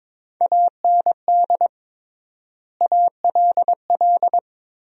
Morse code, heard as a keyed tone, about 22 wpm, 700 Hz. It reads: AND ALL
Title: 3x Word-Spacing - Sentences from Top 100 Words - 22wpm